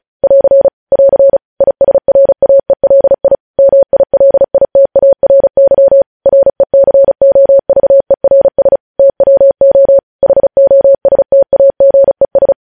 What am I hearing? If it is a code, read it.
Morse code, heard as a keyed tone, about 36 wpm, 550 Hz. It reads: <AR> <AR> ISRAELI MILITARY RECOVERS TWO HOSTAGES
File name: podcast / News headlines-1x-35x35